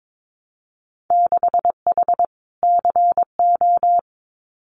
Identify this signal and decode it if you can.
Morse code, heard as a keyed tone, about 22 wpm, 700 Hz. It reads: 6H CO
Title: Winter Field Day Exchanges Encore 22wpm